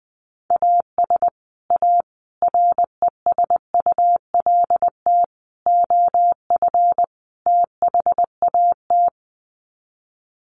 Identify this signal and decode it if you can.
Morse code, heard as a keyed tone, about 20 wpm, 700 Hz. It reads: AS A RESULT OF THAT
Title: Sets of 5 Words Encore 20wpm